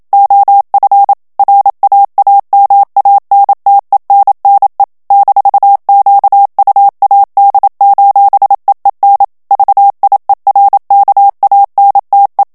Morse code, heard as a keyed tone, about 27 wpm, 800 Hz. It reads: OF RAAMANTENNE -QUAD:EEN VIERKANTE
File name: HOLLANDAIS texte clair 1 vitesse 26 mots minute